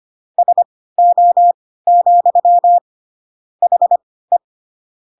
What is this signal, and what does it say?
Morse code, heard as a keyed tone, about 25 wpm, spaced as for 16 wpm, 700 Hz. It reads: SO, HE
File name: Pinocchio / Pinocchio ch3 10 25 ik1hge